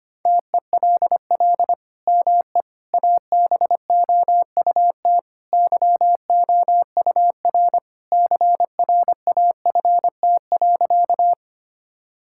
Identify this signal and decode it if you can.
Morse code, heard as a keyed tone, about 25 wpm, 700 Hz. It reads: TELL ME ABOUT YOUR CRAFT.
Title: Sentences from Top 1000 Words - Review Part 6 - 25wpm